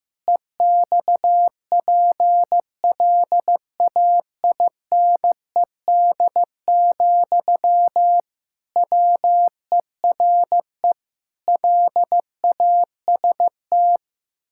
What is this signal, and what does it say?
Morse code, heard as a keyed tone, about 15 wpm, 700 Hz. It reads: EXPLAINED, WERE LAST
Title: Pangram Challenge v2 - 15wpm